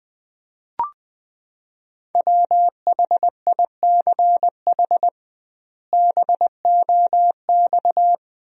Morse code, heard as a keyed tone, about 20 wpm, 700 Hz. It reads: WHICH BOX
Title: 2x Word-Spacing - Sentences from Top 500 Words - 20wpm